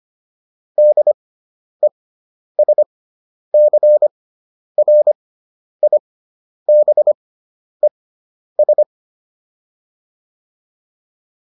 Morse code, heard as a keyed tone, about 25 wpm, spaced as for 10 wpm, 600 Hz. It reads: DESCRIBES